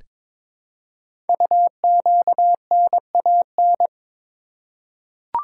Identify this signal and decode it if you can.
Morse code, heard as a keyed tone, about 23 wpm, 700 Hz. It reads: UQNAN
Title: Code Groups - 5 Letters - 22wpm